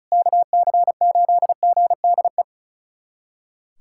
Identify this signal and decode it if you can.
Morse code, heard as a keyed tone, about 36 wpm, 700 Hz. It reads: KC8GDE
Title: USA Callsign DE Callsign - 6 Characters - 35wpm